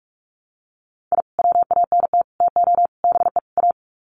Morse code, heard as a keyed tone, about 44 wpm, 700 Hz. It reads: I WANT TO BE A
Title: Sets of 5 Words Encore 45wpm